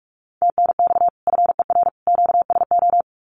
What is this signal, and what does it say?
Morse code, heard as a keyed tone, about 44 wpm, 700 Hz. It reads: TNX FER QSO
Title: Common QSO Phrases 45wpm